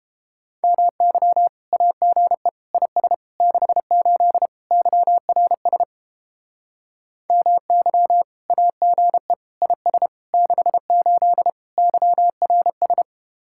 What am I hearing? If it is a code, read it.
Morse code, heard as a keyed tone, about 31 wpm, 700 Hz. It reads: MY AGE IS 68 YRS MY AGE IS 68 YRS